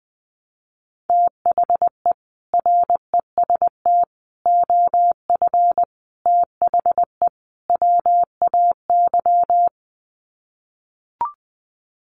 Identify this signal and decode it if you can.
Morse code, heard as a keyed tone, about 20 wpm, 700 Hz. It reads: THE REST OF THE WAY
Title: Sets of 5 Words Encore 20wpm